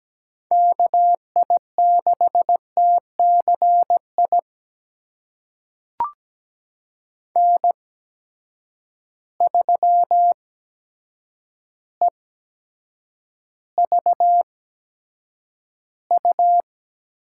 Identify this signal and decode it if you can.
Morse code, heard as a keyed tone, about 17 wpm, spaced as for 8 wpm, 700 Hz. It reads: KI6TCI N3EVU
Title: USA Call Signs - 8x Character Spacing - 17wpm